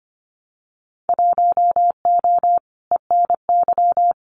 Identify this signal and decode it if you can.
Morse code, heard as a keyed tone, about 25 wpm, 700 Hz. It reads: 1O ENY
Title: Winter Field Day Exchanges 25wpm